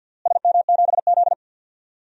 Morse code, heard as a keyed tone, about 52 wpm, 700 Hz. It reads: SM7Z
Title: DXCC-100 Call Signs - 4 Characters - Repeated 3x - 50wpm